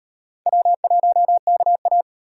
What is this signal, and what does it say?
Morse code, heard as a keyed tone, about 36 wpm, 700 Hz. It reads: W1KA